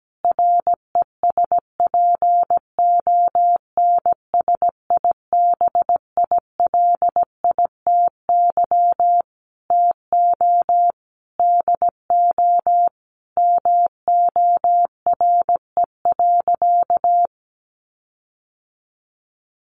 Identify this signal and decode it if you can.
Morse code, heard as a keyed tone, about 17 wpm, 700 Hz. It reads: RESPONSIBILITY TO DO MORE.
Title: Sentences from Top 1400 Words 17wpm